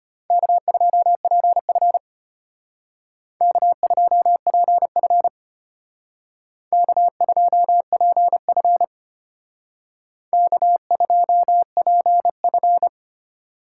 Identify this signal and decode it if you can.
Morse code, heard as a keyed tone, about 32 wpm, 700 Hz. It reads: K2PF K2PF K2PF K2PF